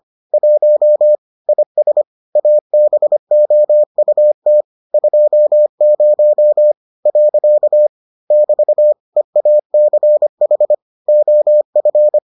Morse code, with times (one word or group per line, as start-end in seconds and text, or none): 0.3-1.1 1
1.5-2.0 IS
2.4-4.6 ABOUT
4.9-7.9 20.
8.3-8.9 <BT>
9.2-10.7 EACH
11.1-12.2 OF